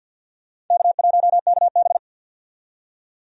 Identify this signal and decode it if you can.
Morse code, heard as a keyed tone, about 47 wpm, 700 Hz. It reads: K1KB
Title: USA Call Signs - 4 Characters - 50wpm